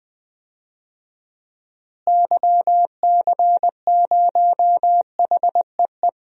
Morse code, heard as a keyed tone, about 21 wpm, 700 Hz. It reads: YC0HEE
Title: DXCC-100 Call Signs - 6 Characters - Repeated 2x - 20wpm